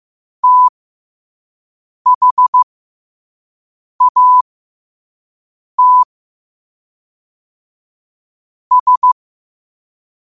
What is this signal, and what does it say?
Morse code, heard as a keyed tone, about 15 wpm, spaced as for 5 wpm, 1000 Hz. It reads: THAT S
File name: qotdfiles / 5wpm